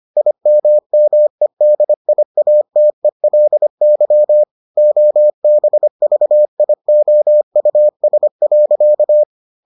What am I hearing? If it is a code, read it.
Morse code, heard as a keyed tone, about 25 wpm, 600 Hz. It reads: IMMEDIATELY OBVIOUS.